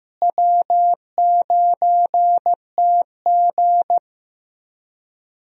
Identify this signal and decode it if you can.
Morse code, heard as a keyed tone, about 16 wpm, 700 Hz. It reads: W9TG
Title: USA Call Signs - 4 Characters - Repeated 2x - 15wpm